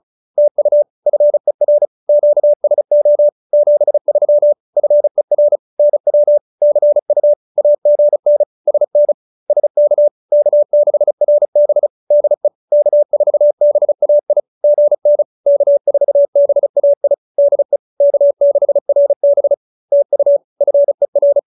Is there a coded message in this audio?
Yes: Morse code from 0.4 to 21.4 s, heard as a keyed tone, about 35 wpm, 590 Hz. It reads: TU FER QSO 73 FER NW CU AGN SN SK K6RB DE K4BAI GN K4BAI DE K6RB TU FER